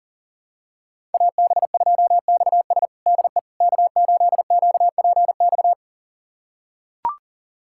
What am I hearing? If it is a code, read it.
Morse code, heard as a keyed tone, about 40 wpm, 700 Hz. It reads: AB2XS DE K8QPX
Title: USA Callsign DE Callsign - 5 Characters - 40wpm